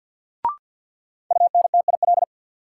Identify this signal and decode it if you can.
Morse code, heard as a keyed tone, about 50 wpm, 700 Hz. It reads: UNTIL
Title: Sets of 5 Words Encore - New Word Drill - 50wpm